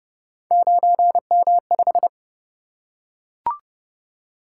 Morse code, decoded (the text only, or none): 9M5